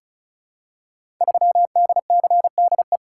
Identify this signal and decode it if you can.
Morse code, heard as a keyed tone, about 36 wpm, 700 Hz. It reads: 3DCDE
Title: Code Groups - 5 Letter:Numbers - 35wpm